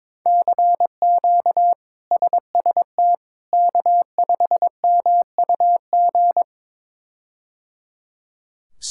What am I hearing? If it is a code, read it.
Morse code, heard as a keyed tone, about 22 wpm, 700 Hz. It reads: CQ SST K5MUG
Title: K1USN SST QSO Practice - 22wpm